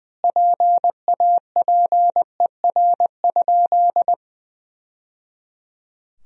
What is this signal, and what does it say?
Morse code, heard as a keyed tone, about 20 wpm, 700 Hz. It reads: PAPER?
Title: Sentences from Top 1000 Words - Review Part 2 - 20wpm